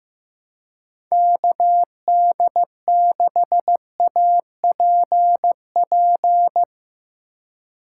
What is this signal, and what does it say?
Morse code, heard as a keyed tone, about 15 wpm, 700 Hz. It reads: KD6APP
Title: USA Call Signs - 6 Characters - 3x Character Spacing - 15wpm